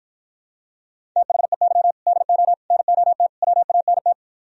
Morse code, heard as a keyed tone, about 52 wpm, 700 Hz. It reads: THEY DO NOT WANT